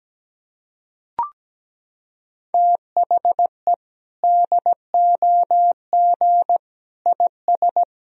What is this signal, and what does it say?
Morse code, heard as a keyed tone, about 17 wpm, 700 Hz. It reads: THE DOG IS
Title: Sentences from Top 400 Words and Abbreviations 17wpm